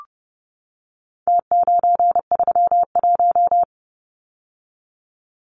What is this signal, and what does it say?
Morse code, heard as a keyed tone, about 30 wpm, 700 Hz. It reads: T931
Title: Numbers - Contest Style - 4-Digit 30wpm